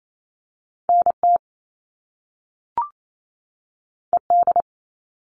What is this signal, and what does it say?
Morse code, heard as a keyed tone, about 28 wpm, 700 Hz. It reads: NT ED
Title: All Cut Numbers - 2-Digit 28wpm